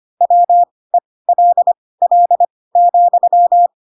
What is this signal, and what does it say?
Morse code, heard as a keyed tone, about 25 wpm, spaced as for 18 wpm, 700 Hz. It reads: WELL,
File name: Pinocchio / Pinocchio ch1 12 25 ik1hge